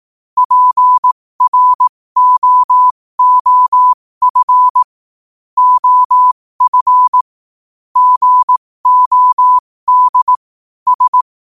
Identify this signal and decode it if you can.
Morse code, heard as a keyed tone, about 18 wpm, spaced as for 15 wpm, 1000 Hz. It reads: PROOF OF GOD S
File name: qotdfiles / 15wpm